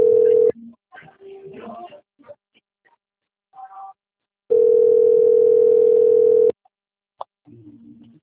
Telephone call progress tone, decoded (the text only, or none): ringback tone